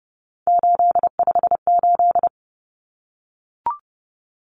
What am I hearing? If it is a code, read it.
Morse code, heard as a keyed tone, about 30 wpm, 700 Hz. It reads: 858
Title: Numbers - 3-Digit 30wpm